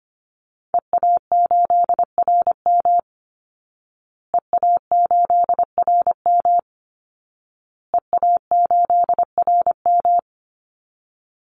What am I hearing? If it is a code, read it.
Morse code, heard as a keyed tone, about 25 wpm, 700 Hz. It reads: EA8RM EA8RM EA8RM
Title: Call Signs Repeated 2x - 25wpm